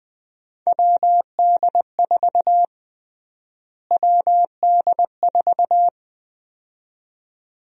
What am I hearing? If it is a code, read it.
Morse code, heard as a keyed tone, about 21 wpm, 700 Hz. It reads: WD4 WD4
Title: USA Call Sign Prefixes - Repeated 2x - 20wpm